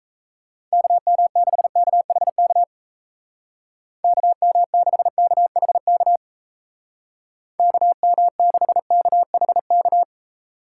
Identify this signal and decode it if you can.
Morse code, heard as a keyed tone, about 37 wpm, 700 Hz. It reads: KM6KHK KM6KHK KM6KHK